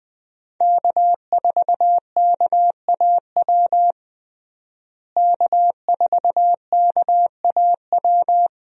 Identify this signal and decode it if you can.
Morse code, heard as a keyed tone, about 20 wpm, 700 Hz. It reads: K4KAW K4KAW
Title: USA Call Signs - 5 Characters - Repeated 2x - 20wpm